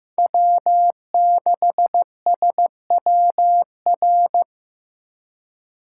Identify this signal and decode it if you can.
Morse code, heard as a keyed tone, about 15 wpm, 700 Hz. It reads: W6SWR
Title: USA Call Signs - 5 Characters - 2x Character Spacing - 15wpm